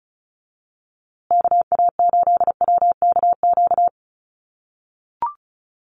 Morse code, heard as a keyed tone, about 34 wpm, 700 Hz. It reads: KA8WKQ
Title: USA Call Signs Repeated 2x 35wpm